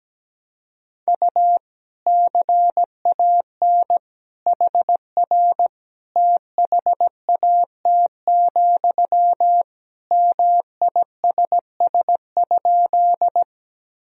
Morse code, heard as a keyed tone, about 17 wpm, 700 Hz. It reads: U CAN HR THAT, MISS?
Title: Sentences from Top 500 Words and Abbreviations 17wpm